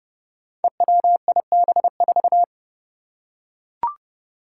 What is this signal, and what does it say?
Morse code, heard as a keyed tone, about 30 wpm, 700 Hz. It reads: EWIB4
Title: Five Challenging Letter-Number Combinations 30wpm